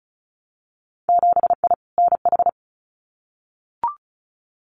Morse code, heard as a keyed tone, about 36 wpm, 700 Hz. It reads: 7I NH
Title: Winter Field Day Exchanges Encore 35wpm